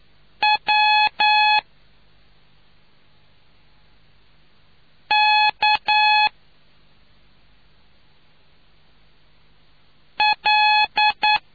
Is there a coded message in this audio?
Yes: Morse code from 0.4 to 11.4 s, heard as a keyed tone, about 9 wpm, 850 Hz. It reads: W K L